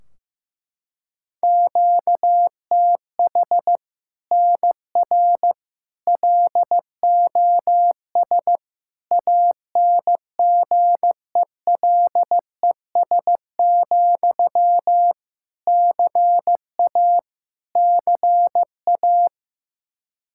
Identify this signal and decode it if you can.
Morse code, heard as a keyed tone, about 15 wpm, 700 Hz. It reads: QTH NR LOS ANGELES, CA CA
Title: Practice QSO - Exchange 1 - Beginning - 15wpm